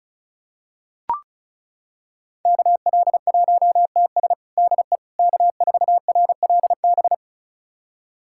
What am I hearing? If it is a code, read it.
Morse code, heard as a keyed tone, about 36 wpm, 700 Hz. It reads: KL1TS DE K4RLB
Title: USA Callsign DE Callsign 35wpm